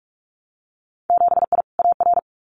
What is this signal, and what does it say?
Morse code, heard as a keyed tone, about 46 wpm, 700 Hz. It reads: 7I AR
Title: Winter Field Day Exchanges Encore 45wpm